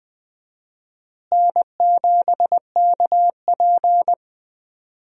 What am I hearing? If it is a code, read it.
Morse code, heard as a keyed tone, about 20 wpm, 700 Hz. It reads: N7KP
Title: USA Call Signs - 4 Characters - Repeated 2x - 20wpm